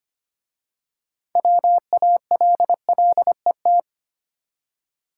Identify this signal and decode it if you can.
Morse code, heard as a keyed tone, about 25 wpm, 690 Hz. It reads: WALLET
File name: practice / 25wpm / Easy Words with Hints - 25wpm